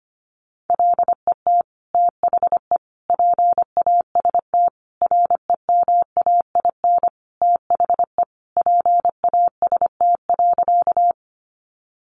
Morse code, heard as a keyed tone, about 25 wpm, 700 Hz. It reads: LET THE PAST REMAIN THE PAST.
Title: Sentences from Top 1200 Words 25wpm